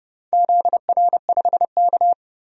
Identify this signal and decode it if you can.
Morse code, heard as a keyed tone, about 30 wpm, 700 Hz. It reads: ZR5K
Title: DXCC-100 Call Signs - 4 Characters - 30wpm